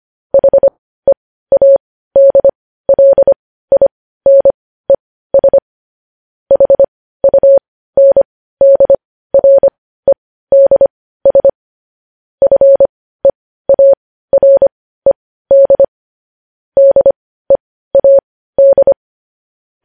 Morse code, heard as a keyed tone, about 25 wpm, spaced as for 15 wpm, 550 Hz. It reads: HEADLINES HUNDREDS FEARED DEAD
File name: podcast / News headlines-1x-25x15